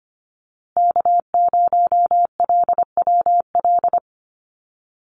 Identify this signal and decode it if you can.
Morse code, heard as a keyed tone, about 25 wpm, 700 Hz. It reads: K0LWL
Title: USA Call Signs Repeated 3x 25wpm